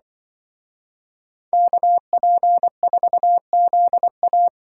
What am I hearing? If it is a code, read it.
Morse code, heard as a keyed tone, about 24 wpm, 700 Hz. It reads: KP4ZA
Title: Speed Racing - USA 2x2 Call Signs 20wpm